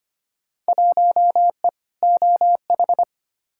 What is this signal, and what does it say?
Morse code, heard as a keyed tone, about 25 wpm, 700 Hz. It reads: 1E OH
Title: Field Day Exchanges - 25wpm